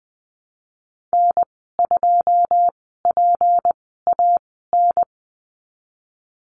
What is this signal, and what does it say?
Morse code, heard as a keyed tone, about 19 wpm, spaced as for 14 wpm, 700 Hz. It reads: N2PAN
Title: 167 Lesson 27g - Number 7 - Call Signs 20wpm